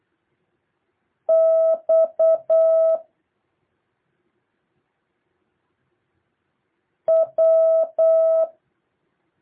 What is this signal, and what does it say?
Morse code, heard as a keyed tone, about 8 wpm, 650 Hz. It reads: X W
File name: jh8cbh / 2023cw